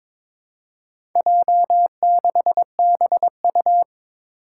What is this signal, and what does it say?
Morse code, heard as a keyed tone, about 21 wpm, 700 Hz. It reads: J6BU
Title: Four Challenging Letter-Number Combinations 22wpm